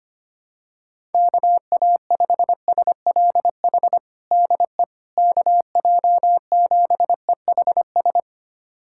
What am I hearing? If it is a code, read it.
Morse code, heard as a keyed tone, about 25 wpm, 700 Hz. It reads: KA5SLH DE KJ7EHS
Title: USA Callsign DE Callsign 25wpm